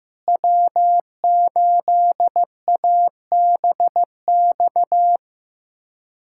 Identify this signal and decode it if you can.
Morse code, heard as a keyed tone, about 15 wpm, 700 Hz. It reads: W8ABX
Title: USA Callsign DE Callsign 15wpm